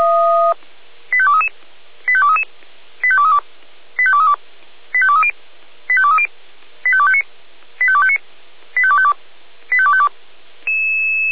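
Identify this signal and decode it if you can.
Five-tone selective call (ZVEI), heard as quick runs of five tones, tones of about 70 ms. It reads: 85233 85233 85232 85232 85230 85230 85259 85259 85252 85252
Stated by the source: ZVEI five-tone call 85885, received on SDR, from the feuerwehr-orlamuende.org fire brigade server.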